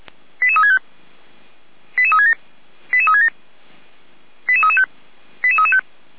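Five-tone selective call (ZVEI), heard as quick runs of five tones, tones of about 70 ms. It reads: 88365 88367 88367 88335 88335
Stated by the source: ZVEI five-tone call received with an SDR.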